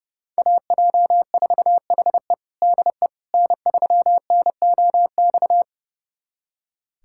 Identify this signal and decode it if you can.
Morse code, heard as a keyed tone, about 30 wpm, 700 Hz. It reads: AJ4HE DE N3NOX
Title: USA Callsign DE Callsign - 5 Characters - 30wpm